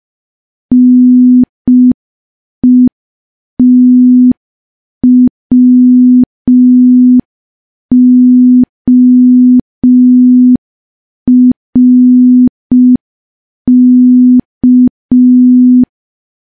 Morse code, heard as a keyed tone, about 5 wpm, 250 Hz. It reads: NETWORK
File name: morse / morse-training-05WPM